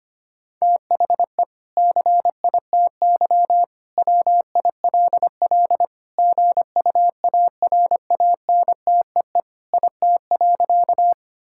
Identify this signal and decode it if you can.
Morse code, heard as a keyed tone, about 25 wpm, 700 Hz. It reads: THE CITY WILL GUARANTEE IT.
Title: Sentences from Top 1600 Words 25wpm